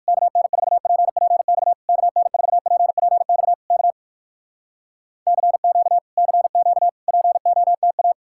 Morse code, heard as a keyed tone, about 69 wpm, spaced as for 59 wpm, 700 Hz. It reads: KN4PPX KN4PPX K CQ CQ POTA